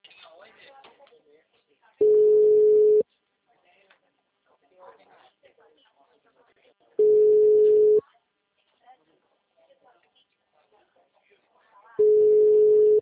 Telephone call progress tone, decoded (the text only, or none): ringback tone